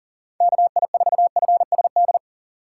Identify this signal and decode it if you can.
Morse code, heard as a keyed tone, about 40 wpm, 700 Hz. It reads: KI4FSD
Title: USA Call Signs - 6 Characters - 6x Character Spacing - 40wpm